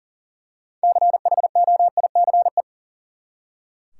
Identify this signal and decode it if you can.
Morse code, heard as a keyed tone, about 37 wpm, 700 Hz. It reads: CHOICE
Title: Six Letter Words - 40wpm